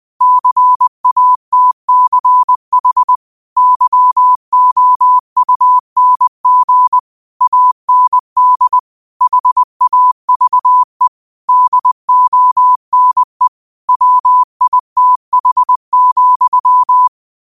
Morse code, heard as a keyed tone, about 20 wpm, 1000 Hz. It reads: CATCH YOUNG AND HAVE DONE WITH,